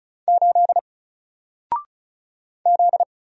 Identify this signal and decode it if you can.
Morse code, heard as a keyed tone, about 36 wpm, 700 Hz. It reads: 8 Z
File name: Single-Letter-Number-35wpm